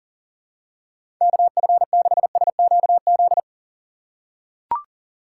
Morse code, heard as a keyed tone, about 40 wpm, 700 Hz. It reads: KF6SQZ
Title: USA Call Signs - 6 Characters - 6x Character Spacing - 40wpm